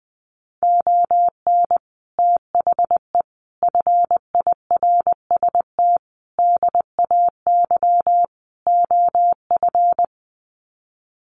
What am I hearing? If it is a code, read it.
Morse code, heard as a keyed tone, about 20 wpm, 700 Hz. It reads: ON THE FIRST DAY OF
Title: Sets of 5 Words Encore 20wpm